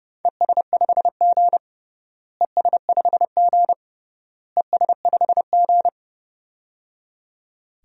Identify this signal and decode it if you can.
Morse code, heard as a keyed tone, about 30 wpm, 700 Hz. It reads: ES5G ES5G ES5G
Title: DXCC-100 Call Signs - 4 Characters - Repeated 3x - 30wpm